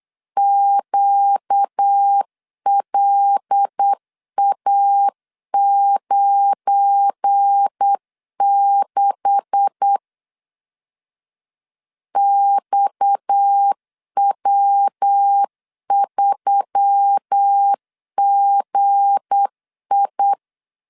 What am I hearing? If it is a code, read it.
Morse code, heard as a keyed tone, about 8 wpm, 800 Hz. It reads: QLA96 XW3GI
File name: LEZIONI / 9 286 8wpm 3425